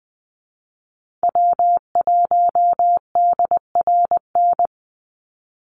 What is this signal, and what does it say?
Morse code, heard as a keyed tone, about 20 wpm, 700 Hz. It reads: W1DRN
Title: USA Call Signs 20wpm